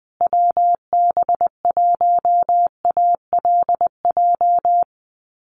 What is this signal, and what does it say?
Morse code, heard as a keyed tone, about 20 wpm, 700 Hz. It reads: WB1ALJ
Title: USA Call Signs Repeated 2x 20wpm